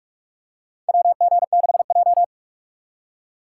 Morse code, heard as a keyed tone, about 44 wpm, 700 Hz. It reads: WG6J
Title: USA Call Signs - 4 Characters - 4x Character Spacing - 45wpm